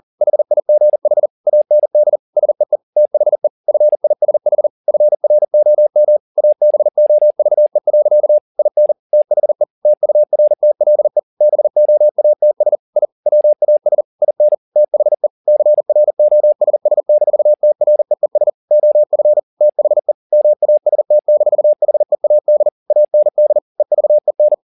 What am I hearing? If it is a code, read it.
Morse code, heard as a keyed tone, about 40 wpm, 600 Hz. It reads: HIGH AND SEE THE FISH FROM ABOVE. IN THE TURTLE BOATS I WAS IN THE CROSS-TREES OF THE MAST-HEAD AND EVEN
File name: the old man and the sea 2 40wpm